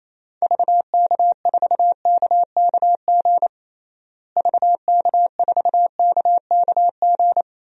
Morse code, heard as a keyed tone, about 28 wpm, 700 Hz. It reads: VK4KKG VK4KKG
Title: DXCC-100 Call Signs - 6 Characters - Repeated 2x - 28wpm